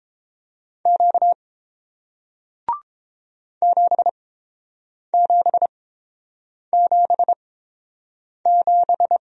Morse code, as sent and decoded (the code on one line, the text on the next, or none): --.- / --... --... --... --...
Q 7777